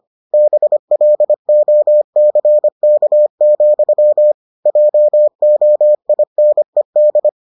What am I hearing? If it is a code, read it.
Morse code, heard as a keyed tone, about 25 wpm, 600 Hz. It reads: BLOCK, JOINED